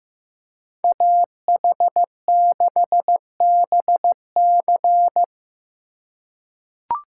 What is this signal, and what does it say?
Morse code, heard as a keyed tone, about 15 wpm, 700 Hz. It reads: AH6BC
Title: USA Call Signs - 5 Characters - 3x Character Spacing - 15wpm